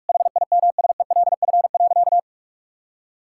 Morse code, heard as a keyed tone, about 47 wpm, 700 Hz. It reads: HIMSELF.
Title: Sentences from Top 1600 Words 45wpm